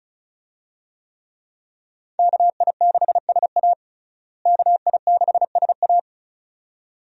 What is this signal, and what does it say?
Morse code, heard as a keyed tone, about 36 wpm, 700 Hz. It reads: KI6SA KI6SA
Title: USA Call Signs - 5 Characters - Repeated 2x - 35wpm